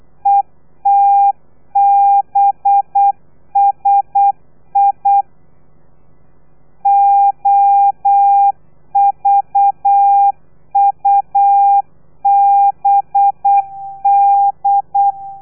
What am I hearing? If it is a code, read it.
Morse code, heard as a keyed tone, about 8 wpm, 800 Hz. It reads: ETBSI OVUBD
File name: telegrafia / i2tfj / Lez4 35